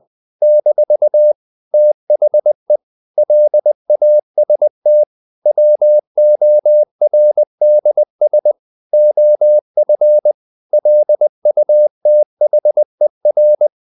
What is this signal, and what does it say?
Morse code, heard as a keyed tone, about 20 wpm, 600 Hz. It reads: - THE LAST WORDS OF LUTHER
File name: CW / Practice / EnglishQuotations0320